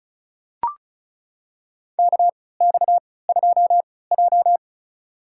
Morse code, heard as a keyed tone, about 36 wpm, spaced as for 20 wpm, 700 Hz. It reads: KX2J
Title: USA Call Signs - 4 Characters - 3x Character Spacing - 35wpm